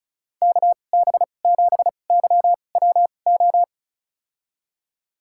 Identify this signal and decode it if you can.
Morse code, heard as a keyed tone, about 34 wpm, spaced as for 25 wpm, 700 Hz. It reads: KB7YWO
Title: USA Call Signs - 6 Characters - 2x Character Spacing - 35wpm